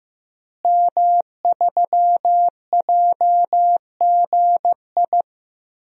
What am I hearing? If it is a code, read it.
Morse code, heard as a keyed tone, about 16 wpm, 700 Hz. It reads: M3JGI
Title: DXCC-100 Call Signs - 5 Characters - Repeated 2x - 15wpm